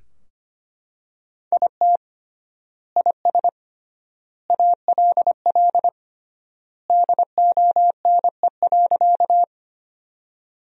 Morse code, heard as a keyed tone, about 25 wpm, 700 Hz. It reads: IT IS ALL DONE.
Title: 3x Word-Spacing - Sentences from Top 500 Words - 25wpm